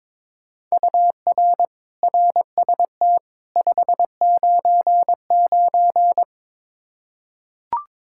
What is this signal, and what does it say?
Morse code, heard as a keyed tone, about 22 wpm, 700 Hz. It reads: UR RST 599
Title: Practice QSO - Exchange 1 - Beginning - 22wpm